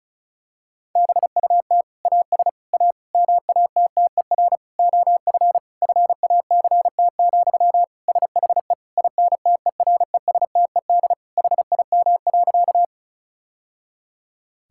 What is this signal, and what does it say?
Morse code, heard as a keyed tone, about 36 wpm, 700 Hz. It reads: BUT AS A MATTER OF FACT, SHE INTERESTED HIM.
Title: Sentences from Top 1000 Words - Review Part 6 - 35wpm